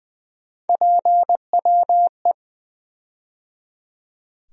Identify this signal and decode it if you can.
Morse code, heard as a keyed tone, about 21 wpm, 700 Hz. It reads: PWE
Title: USA Call Sign Postfixes - 3 Letters 20wpm